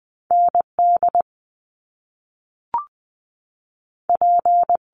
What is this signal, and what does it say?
Morse code, heard as a keyed tone, about 20 wpm, 700 Hz. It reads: N D P
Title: Top Two Letter Combinations in Words 20wpm